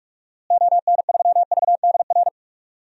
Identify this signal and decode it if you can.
Morse code, heard as a keyed tone, about 46 wpm, 700 Hz. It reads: ON3VDR
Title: DXCC-100 Call Signs - 6 Characters - Repeated 2x - 45wpm